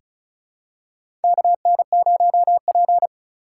Morse code, heard as a keyed tone, about 36 wpm, 700 Hz. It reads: KN0P
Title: USA Call Signs - 4 Characters - 4x Character Spacing - 35wpm